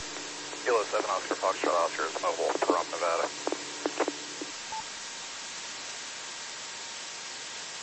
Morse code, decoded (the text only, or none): none